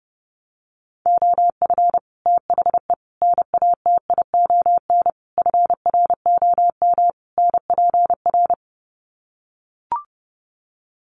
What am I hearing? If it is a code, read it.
Morse code, heard as a keyed tone, about 30 wpm, 700 Hz. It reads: OF THE NATION FROM NPR
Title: Sets of 5 Words 30wpm